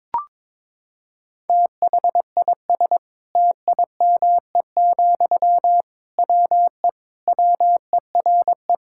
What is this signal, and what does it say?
Morse code, heard as a keyed tone, about 22 wpm, 700 Hz. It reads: THIS TIME, WE WERE
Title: Sentences from Top 1000 Words - Review Part 6 - 22wpm